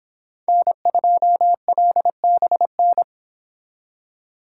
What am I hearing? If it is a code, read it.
Morse code, heard as a keyed tone, about 28 wpm, 700 Hz. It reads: N2LBN